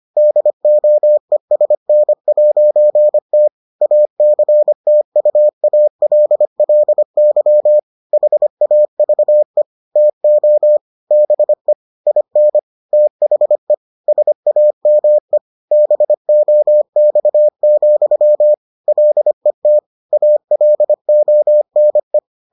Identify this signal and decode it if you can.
Morse code, heard as a keyed tone, about 25 wpm, 600 Hz. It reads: DOESN'T ACTUALLY HAVE TO BE IN THE SAME BOX, LET ALONE